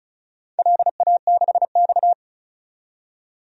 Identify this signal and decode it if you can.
Morse code, heard as a keyed tone, about 36 wpm, 690 Hz. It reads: LA6X
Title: DXCC-100 Call Signs - 4 Characters - 35wpm